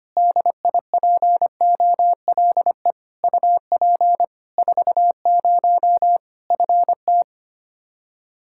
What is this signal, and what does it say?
Morse code, heard as a keyed tone, about 25 wpm, 700 Hz. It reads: DIPOLE UP 40 FT
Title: Practice QSO - Exchange 2 - Info - 25wpm